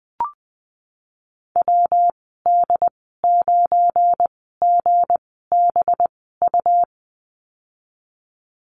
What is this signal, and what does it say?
Morse code, heard as a keyed tone, about 20 wpm, spaced as for 14 wpm, 700 Hz. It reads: WD9GBU
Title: 186 Lesson 30g - Letter b - Call Signs 20wpm